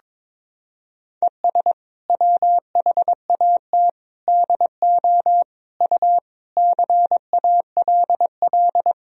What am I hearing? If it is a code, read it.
Morse code, heard as a keyed tone, about 22 wpm, 700 Hz. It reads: ES WHAT DO U CALL
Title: Sentences from Top 100 Words and Abbreviations 22wpm